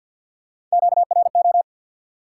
Morse code, heard as a keyed tone, about 54 wpm, 700 Hz. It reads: QRO
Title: Top 90 QSO Elements 50wpm